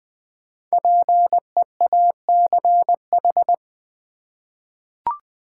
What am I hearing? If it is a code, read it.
Morse code, heard as a keyed tone, about 20 wpm, 700 Hz. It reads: PEACH
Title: Difficult Words with Hints - 20wpm